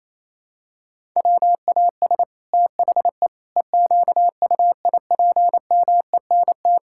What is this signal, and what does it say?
Morse code, heard as a keyed tone, about 28 wpm, 700 Hz. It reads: WAS THE EQUIPMENT
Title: Sentences from Top 1000 Words - Review Part 5 - 28wpm